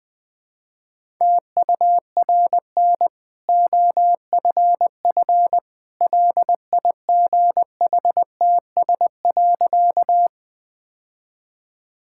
Morse code, heard as a keyed tone, about 21 wpm, 700 Hz. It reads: TURN OFF LIGHTS.
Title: Easy Sentences with Hints - 20wpm